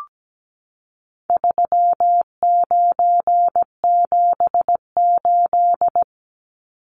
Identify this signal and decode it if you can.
Morse code, heard as a keyed tone, about 17 wpm, 700 Hz. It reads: 3978
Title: Numbers - 4-Digit 17wpm